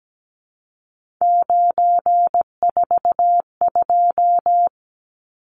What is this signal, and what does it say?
Morse code, heard as a keyed tone, about 17 wpm, 700 Hz. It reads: 942
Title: Numbers - 3-Digit 17wpm